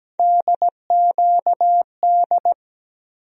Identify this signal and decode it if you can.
Morse code, heard as a keyed tone, about 17 wpm, 700 Hz. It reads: DQD
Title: USA Call Sign Postfixes - 3 Letters 17wpm